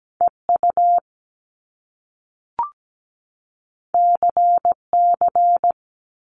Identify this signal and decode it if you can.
Morse code, heard as a keyed tone, about 17 wpm, 700 Hz. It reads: EU CC
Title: Top Two Letter Combinations in Words 17wpm